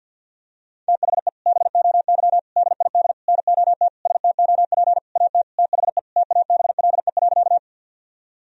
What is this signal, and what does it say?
Morse code, heard as a keyed tone, about 53 wpm, 700 Hz. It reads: THE BOY DID NOT STOP AT THE TABLE.
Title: Sentences from Top 1000 Words - Review Part 2 - 50wpm